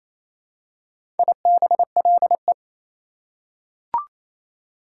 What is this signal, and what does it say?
Morse code, heard as a keyed tone, about 28 wpm, 700 Hz. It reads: IBLE
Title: Prefixes and Suffixes 28wpm